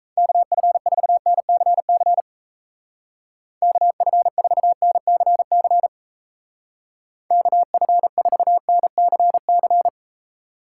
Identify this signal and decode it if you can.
Morse code, heard as a keyed tone, about 34 wpm, 700 Hz. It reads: KF4NCC KF4NCC KF4NCC
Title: Speed Racing - USA 2x3 Call Signs 28wpm